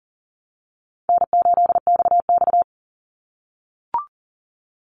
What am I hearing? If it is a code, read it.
Morse code, heard as a keyed tone, about 40 wpm, 700 Hz. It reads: N8XX
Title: Easy Call Signs - 40wpm